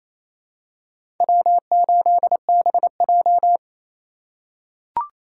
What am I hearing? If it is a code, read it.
Morse code, heard as a keyed tone, about 28 wpm, 700 Hz. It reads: W8BJ
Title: USA Call Signs - 4 Characters - 4x Character Spacing - 28wpm